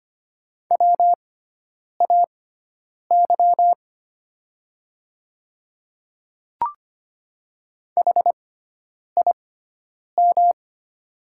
Morse code, heard as a keyed tone, about 25 wpm, spaced as for 9 wpm, 700 Hz. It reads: WAY HIM
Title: 6x Character Spacing - Top 200 Words - 25wpm